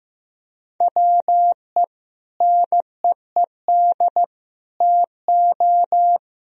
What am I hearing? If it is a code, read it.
Morse code, heard as a keyed tone, about 15 wpm, 700 Hz. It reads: WE NEED TO
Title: Sentences from Top 1000 Words - Review Part 4 - 15wpm